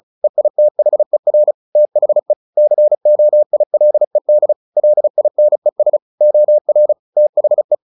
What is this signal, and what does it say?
Morse code, heard as a keyed tone, about 34 wpm, 600 Hz. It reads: EITHER THE COILED LINES OR THE
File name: the old man and the sea / the old man and the sea 1 35wpm